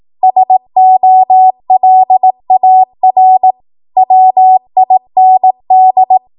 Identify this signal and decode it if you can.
Morse code, heard as a keyed tone, about 18 wpm, 750 Hz. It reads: SOLAR WIND
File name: Morse / 18WPM